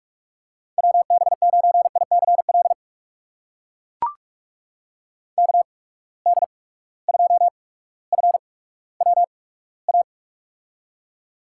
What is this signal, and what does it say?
Morse code, heard as a keyed tone, about 43 wpm, spaced as for 19 wpm, 700 Hz. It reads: WB9ICL KD2FWA
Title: USA Call Signs - 6 Characters - 8x Character Spacing - 45wpm